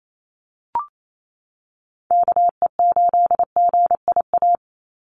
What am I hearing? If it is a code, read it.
Morse code, heard as a keyed tone, about 28 wpm, 700 Hz. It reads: KE8GIA
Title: USA Call Signs 28wpm